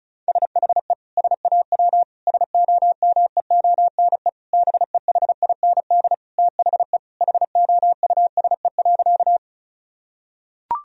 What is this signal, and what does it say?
Morse code, heard as a keyed tone, about 36 wpm, 700 Hz. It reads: SHE SAW SOMEONE BEHIND THE HOUSE.
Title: Sentences from Top 1000 Words - Review Part 4 - 35wpm